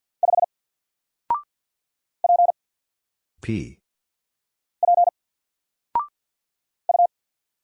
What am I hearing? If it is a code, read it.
Morse code, heard as a keyed tone, about 49 wpm, 700 Hz. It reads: 5 P P U
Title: Single Letter-Number Alt 50wpm